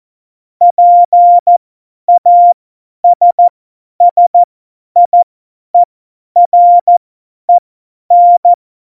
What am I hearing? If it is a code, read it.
Morse code, heard as a keyed tone, about 14 wpm, spaced as for 10 wpm, 700 Hz. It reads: PASSIEREN
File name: Morsekurs / Witze / Witz 29. Dezember 2020, 50 F70